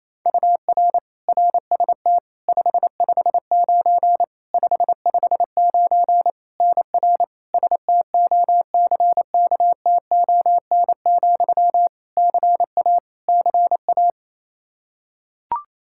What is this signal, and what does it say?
Morse code, heard as a keyed tone, about 28 wpm, 700 Hz. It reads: UR RST 559 559 NR STOCKTON, CA CA
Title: Practice QSO - Exchange 1 - Beginning - 28wpm